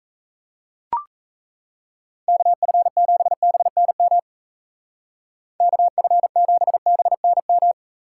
Morse code, heard as a keyed tone, about 33 wpm, 700 Hz. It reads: KF7BNM KF7BNM
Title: Speed Racing - USA 2x3 Call Signs 28wpm